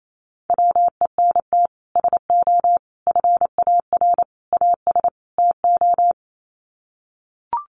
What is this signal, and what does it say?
Morse code, heard as a keyed tone, about 28 wpm, 700 Hz. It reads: WENT SO FAR AS TO
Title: Sets of 5 Words Encore 28wpm